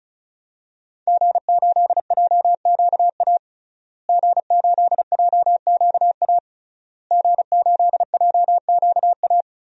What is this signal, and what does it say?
Morse code, heard as a keyed tone, about 36 wpm, 700 Hz. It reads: G8JQA G8JQA G8JQA
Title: DXCC-100 Call Signs - 5 Characters - Repeated 3x - 35wpm